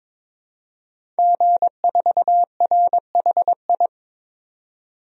Morse code, heard as a keyed tone, about 22 wpm, 700 Hz. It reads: G4RHI
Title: DXCC-100 Call Signs - 5 Characters - 22wpm